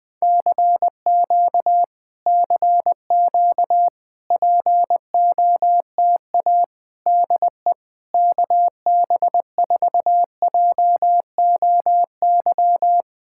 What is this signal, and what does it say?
Morse code, heard as a keyed tone, about 20 wpm, 700 Hz. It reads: CQ CQ POTA DE KB4JOY
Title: POTA QSO Practice - 20wpm